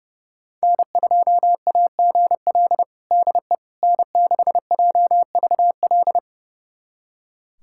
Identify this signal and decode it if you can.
Morse code, heard as a keyed tone, about 30 wpm, 700 Hz. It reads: N2AGL DE N6JVL
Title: USA Callsign DE Callsign - 5 Characters - 30wpm